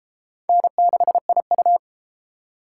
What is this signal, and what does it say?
Morse code, heard as a keyed tone, about 31 wpm, 700 Hz. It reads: N6IU